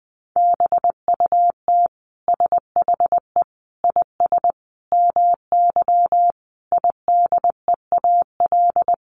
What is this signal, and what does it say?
Morse code, heard as a keyed tone, about 20 wpm, 700 Hz. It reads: BUT SHE IS MY IDEAL
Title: Sentences from Top 1400 Words 20wpm